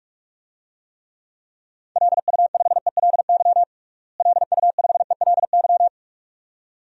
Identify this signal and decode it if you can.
Morse code, heard as a keyed tone, about 41 wpm, 700 Hz. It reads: LU5ELY LU5ELY
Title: DXCC-100 Call Signs - 6 Characters - Repeated 2x - 45wpm